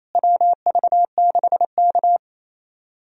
Morse code, heard as a keyed tone, about 27 wpm, 700 Hz. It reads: WV6K
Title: USA Call Signs - 4 Characters - Repeated 2x - 28wpm